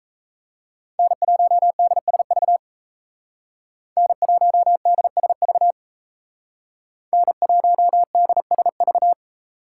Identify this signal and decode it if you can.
Morse code, heard as a keyed tone, about 33 wpm, 700 Hz. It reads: N1DSV N1DSV N1DSV
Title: Speed Racing - USA 1x3 Call Signs 28wpm